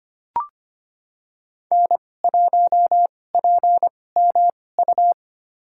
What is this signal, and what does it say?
Morse code, heard as a keyed tone, about 25 wpm, spaced as for 18 wpm, 700 Hz. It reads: N1PMU
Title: USA Call Signs - 2x Character Spacing - 25wpm